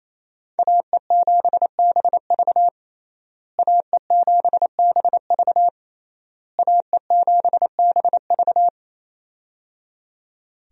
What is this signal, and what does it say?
Morse code, heard as a keyed tone, about 28 wpm, 700 Hz. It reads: AE7BV AE7BV AE7BV
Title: USA Call Signs - 5 Characters - Repeated 3x - 28wpm